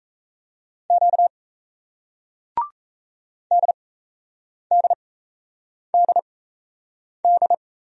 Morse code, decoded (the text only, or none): Q DDDD